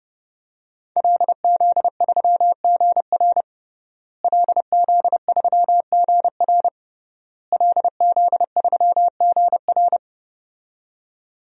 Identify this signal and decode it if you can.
Morse code, heard as a keyed tone, about 32 wpm, 700 Hz. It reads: LZ3GR LZ3GR LZ3GR
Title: DXCC-100 Call Signs - 5 Characters - Repeated 3x - 30wpm